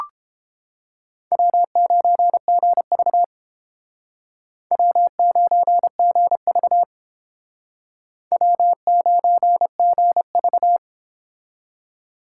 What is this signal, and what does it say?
Morse code, heard as a keyed tone, about 31 wpm, 700 Hz. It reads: W9GV W9GV W9GV